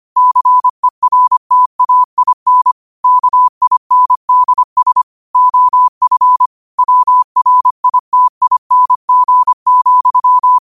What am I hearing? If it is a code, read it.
Morse code, heard as a keyed tone, about 25 wpm, 1000 Hz. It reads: CERTAIN KINDS OF WRITING,
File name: qotdfiles / 25wpm